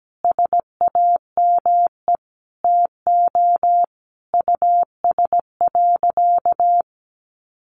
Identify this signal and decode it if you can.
Morse code, heard as a keyed tone, about 17 wpm, 700 Hz. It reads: SAME TO US.
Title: Sentences from Top 600 Words 17wpm